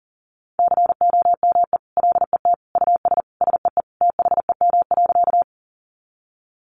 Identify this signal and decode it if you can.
Morse code, heard as a keyed tone, about 40 wpm, 700 Hz. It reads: COME LET US SEE THEM.
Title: Sentences from Top 300 Words 40wpm